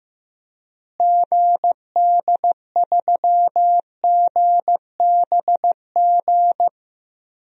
Morse code, heard as a keyed tone, about 15 wpm, 700 Hz. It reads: GD3GBG
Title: DXCC-100 Call Signs - 6 Characters - Repeated 3x - 15wpm